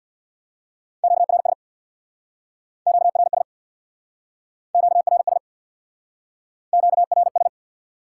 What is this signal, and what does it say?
Morse code, heard as a keyed tone, about 69 wpm, 700 Hz. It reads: QRS QRS QRS QRS